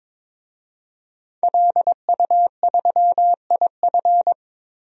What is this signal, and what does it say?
Morse code, heard as a keyed tone, about 22 wpm, 700 Hz. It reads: LU3IF
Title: DXCC-100 Call Signs - 5 Characters - 22wpm